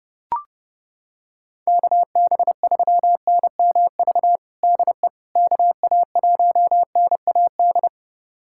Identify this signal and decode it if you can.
Morse code, heard as a keyed tone, about 30 wpm, 700 Hz. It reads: KB3NMV DE KA1NAD
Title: USA Callsign DE Callsign - 6 Characters - 30wpm